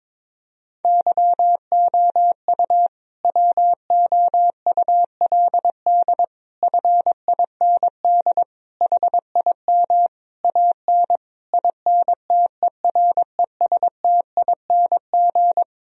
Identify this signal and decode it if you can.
Morse code, heard as a keyed tone, about 23 wpm, 700 Hz. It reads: YOU WOULD FIND HIM AN INTERESTING
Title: Sentences from Top 1000 Words - Review Part 1 - 22wpm